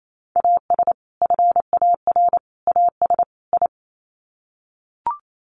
Morse code, heard as a keyed tone, about 28 wpm, 700 Hz. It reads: AS FAR AS I
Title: Sets of 4 Words 28wpm